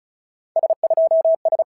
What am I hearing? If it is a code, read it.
Morse code, heard as a keyed tone, about 34 wpm, 650 Hz. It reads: S2S
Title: SOTA QSO Practice 35wpm